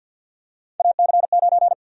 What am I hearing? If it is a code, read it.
Morse code, heard as a keyed tone, about 46 wpm, 700 Hz. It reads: AC9
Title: USA Call Sign Prefixes - 50wpm